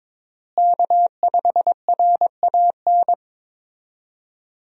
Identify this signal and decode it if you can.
Morse code, heard as a keyed tone, about 22 wpm, 700 Hz. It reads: K5RAN
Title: USA Call Signs - 5 Characters - Repeated 3x - 22wpm